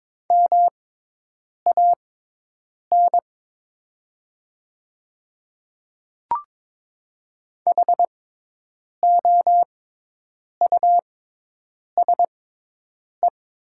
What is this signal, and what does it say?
Morse code, heard as a keyed tone, about 22 wpm, spaced as for 8 wpm, 700 Hz. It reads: MAN HOUSE
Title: 6x Character Spacing - Top 200 Words - 22wpm